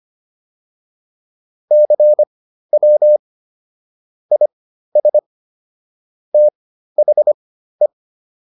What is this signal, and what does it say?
Morse code, heard as a keyed tone, about 25 wpm, spaced as for 13 wpm, 600 Hz. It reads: CW IS THE